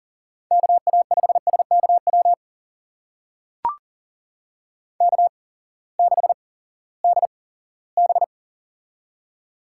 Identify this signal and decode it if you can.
Morse code, heard as a keyed tone, about 41 wpm, spaced as for 20 wpm, 700 Hz. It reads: KA5SKW K6DB
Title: USA Call Signs - 8x Character Spacing - 40wpm